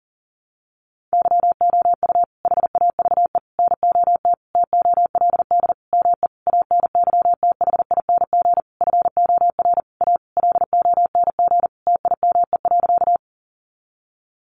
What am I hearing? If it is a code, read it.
Morse code, heard as a keyed tone, about 40 wpm, 700 Hz. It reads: YOU HAVE NOT TOLD ME ANYTHING FOR A LONG TIME.
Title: Sentences from Top 1000 Words 40wpm